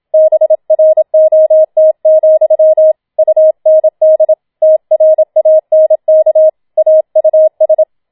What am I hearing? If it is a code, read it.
Morse code, heard as a keyed tone, about 27 wpm, 630 Hz. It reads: BROT, UND TRANK AUS